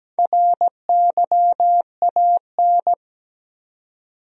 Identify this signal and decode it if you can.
Morse code, heard as a keyed tone, about 17 wpm, 700 Hz. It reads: RYAN
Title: USA Common Names - Max 4 Letters - 17wpm